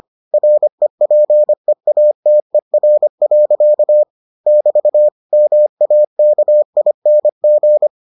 Morse code, heard as a keyed tone, about 25 wpm, 600 Hz. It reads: REPEATER. <BT> MAKING